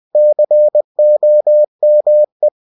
Morse code, heard as a keyed tone, about 20 wpm, 600 Hz. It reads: COME
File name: cw-books / winnie-the-pooh / winnie-20wpm-page-0080